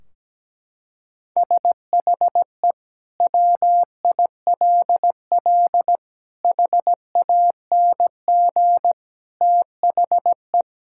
Morse code, heard as a keyed tone, about 17 wpm, 700 Hz. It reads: SHE WILL HANG THE
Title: Sentences from Top 1600 Words 17wpm